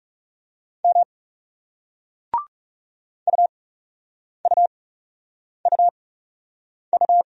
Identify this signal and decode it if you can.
Morse code, heard as a keyed tone, about 40 wpm, 700 Hz. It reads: M U U U U